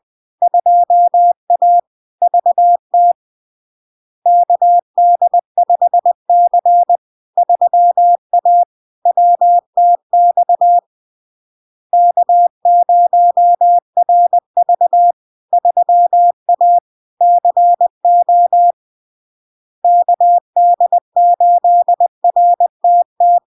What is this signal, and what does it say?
Morse code, heard as a keyed tone, about 20 wpm, 700 Hz. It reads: 2A VT KD5C 3A WTX K0RV 3A CO KD8RTT
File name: LARK-FD-2022-Log-in-CW-20wpm-700-Hz